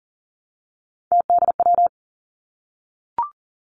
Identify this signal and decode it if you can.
Morse code, heard as a keyed tone, about 43 wpm, 700 Hz. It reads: TDW
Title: Three Challenging Letter-Number Combinations 40wpm